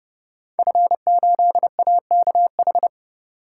Morse code, heard as a keyed tone, about 29 wpm, 700 Hz. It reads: F8AKH
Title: Code Groups - 5 Letter:Numbers - 30wpm